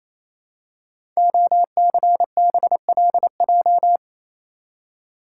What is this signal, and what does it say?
Morse code, heard as a keyed tone, about 27 wpm, 700 Hz. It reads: OCBLJ
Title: Code Groups - 5 Letters - 28wpm